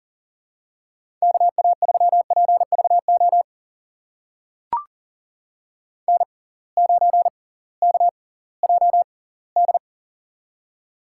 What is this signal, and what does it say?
Morse code, heard as a keyed tone, about 40 wpm, spaced as for 22 wpm, 700 Hz. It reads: KA3PVO N9KJD
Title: USA Call Signs - 6x Character Spacing - 40wpm